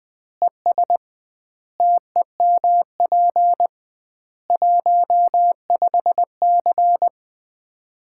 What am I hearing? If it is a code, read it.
Morse code, heard as a keyed tone, about 21 wpm, 700 Hz. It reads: ES TEMP 15C
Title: Common QSO Phrases - 2x Word-Spacing - 20wpm